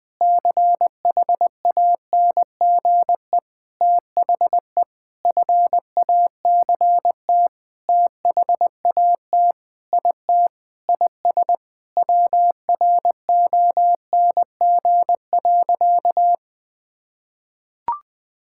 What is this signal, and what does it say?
Morse code, heard as a keyed tone, about 20 wpm, 700 Hz. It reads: CHANGE THE FACT THAT IT IS WRONG.
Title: Sentences with 7-Word Phrases - Spoken Beginning - 20wpm